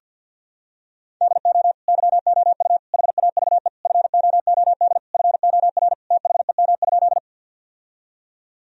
Morse code, heard as a keyed tone, about 46 wpm, 700 Hz. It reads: DO YOU HAVE FOOD FOR THEM?
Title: Sentences from Top 1000 Words - Review Part 1 - 50wpm